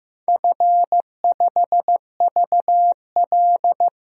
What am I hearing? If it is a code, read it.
Morse code, heard as a keyed tone, about 15 wpm, 700 Hz. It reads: F5VL
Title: DXCC-100 Call Signs - 4 Characters - 15wpm